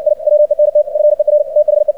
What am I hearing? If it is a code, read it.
Morse code, heard as a keyed tone, about 60 wpm, 600 Hz. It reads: I WENT FER THE
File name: archived / CRSnet